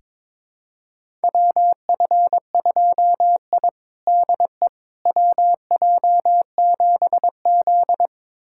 Morse code, heard as a keyed tone, about 22 wpm, 700 Hz. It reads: WF2I DE WJ7Z
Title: USA Callsign DE Callsign - 4 Characters - 22wpm